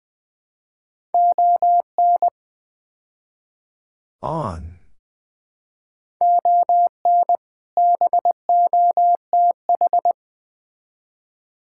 Morse code, heard as a keyed tone, about 20 wpm, 700 Hz. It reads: ON ON BOTH
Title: Build A Phrase - 5-Word Phrases - 20wpm